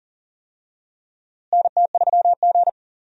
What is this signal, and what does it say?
Morse code, heard as a keyed tone, about 40 wpm, 700 Hz. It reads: NT3G
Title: USA Call Signs - 4 Characters - Repeated 2x - 40wpm